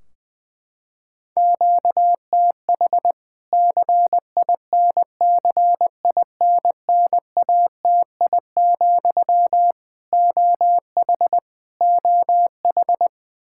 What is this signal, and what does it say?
Morse code, heard as a keyed tone, about 21 wpm, 700 Hz. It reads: QTH CINCINNATI, OH OH
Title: K1USN SST QSO Practice 20wpm